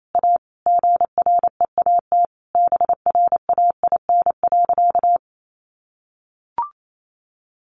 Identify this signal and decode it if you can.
Morse code, heard as a keyed tone, about 28 wpm, 700 Hz. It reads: A GREAT BRAIN.
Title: Sentences from Top 1100 Words 28wpm